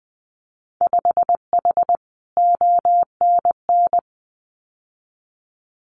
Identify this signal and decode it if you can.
Morse code, heard as a keyed tone, about 20 wpm, 700 Hz. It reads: 5H ONN
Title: Winter Field Day Exchanges Encore 20wpm